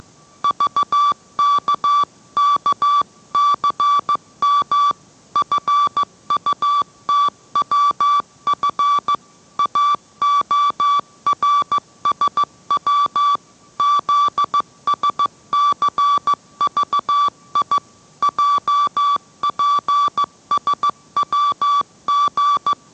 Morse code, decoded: VKKCM FUTWF AORSW ZSCVI JPSWG

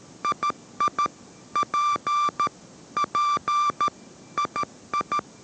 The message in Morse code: II P P II

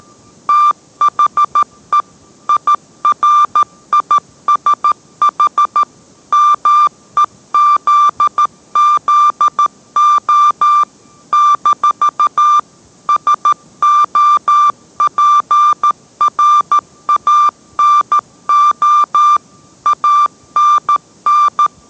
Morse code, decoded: THE IRISH MEZZO-SOPRANO ANN

